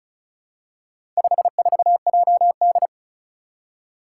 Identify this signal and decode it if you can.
Morse code, heard as a keyed tone, about 36 wpm, 700 Hz. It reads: 54JD
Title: Four Challenging Letter-Number Combinations 35wpm